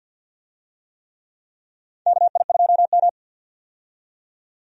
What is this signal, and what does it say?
Morse code, heard as a keyed tone, about 51 wpm, 700 Hz. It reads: KI2M
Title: USA Call Signs - 4 Characters - 50wpm